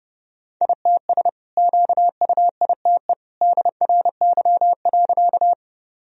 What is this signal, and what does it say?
Morse code, heard as a keyed tone, about 32 wpm, 700 Hz. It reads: ITS QUITE DRY.
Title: Sentences from Top 1000 Words - Review Part 3 - 30wpm